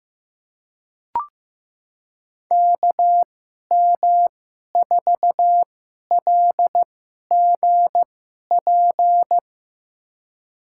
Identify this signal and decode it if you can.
Morse code, heard as a keyed tone, about 15 wpm, spaced as for 11 wpm, 700 Hz. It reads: KM4LGP
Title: USA Call Signs - 6 Characters - 2x Character Spacing - 15wpm